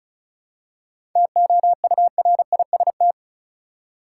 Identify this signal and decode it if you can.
Morse code, heard as a keyed tone, about 34 wpm, 700 Hz. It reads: TOURIST
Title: Seven Letter Words - 35wpm